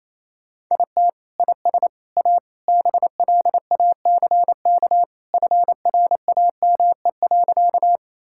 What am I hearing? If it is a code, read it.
Morse code, heard as a keyed tone, about 28 wpm, 700 Hz. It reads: IT IS A BLACK FRAME.
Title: Sentences from Top 1000 Words - Review Part 6 - 28wpm